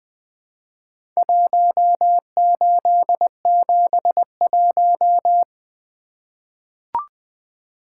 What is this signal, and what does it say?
Morse code, heard as a keyed tone, about 20 wpm, 700 Hz. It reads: 1871
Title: Four Letter-Number Combinations - 20wpm